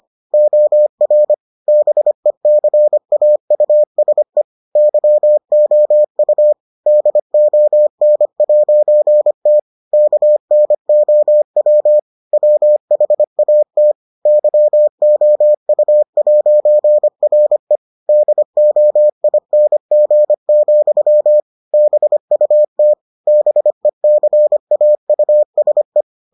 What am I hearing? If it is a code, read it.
Morse code, heard as a keyed tone, about 25 wpm, 600 Hz. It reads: OR BECAUSE YOU DON'T KNOW WHAT YOU'RE DOING, BUT BECAUSE